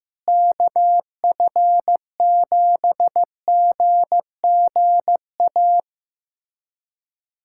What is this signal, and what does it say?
Morse code, heard as a keyed tone, about 15 wpm, 700 Hz. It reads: KF7GGA
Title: USA Callsign DE Callsign 15wpm